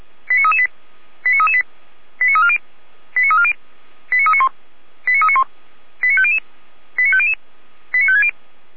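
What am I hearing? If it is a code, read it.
Five-tone selective call (ZVEI), heard as quick runs of five tones, tones of about 70 ms. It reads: 89338 89338 89340 89340 89391 89391 89500 89500 89560
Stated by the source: ZVEI five-tone selective call 89320 received by SDR.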